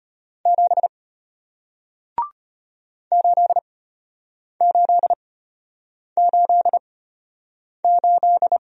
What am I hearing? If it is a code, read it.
Morse code, heard as a keyed tone, about 34 wpm, spaced as for 8 wpm, 690 Hz. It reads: 7 8888